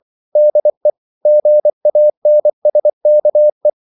Morse code, heard as a keyed tone, about 24 wpm, 600 Hz. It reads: DE GANSKE
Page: Ferie 24 wpm